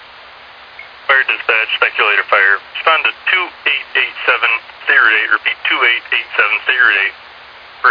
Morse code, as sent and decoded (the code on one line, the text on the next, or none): none